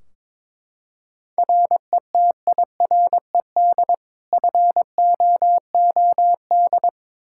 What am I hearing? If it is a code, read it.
Morse code, heard as a keyed tone, about 22 wpm, 700 Hz. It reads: RETIRED FOOD
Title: Practice QSO - Exchange 3 - More Info - 22wpm